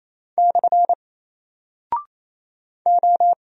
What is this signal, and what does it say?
Morse code, heard as a keyed tone, about 27 wpm, 700 Hz. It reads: / O
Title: Single Character (Letter:Number:Punctuation:Prosign) Alt 28wpm